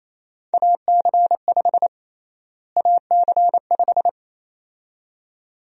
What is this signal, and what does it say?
Morse code, heard as a keyed tone, about 28 wpm, 700 Hz. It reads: AC5 AC5
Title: USA Call Sign Prefixes - Repeated 2x - 28wpm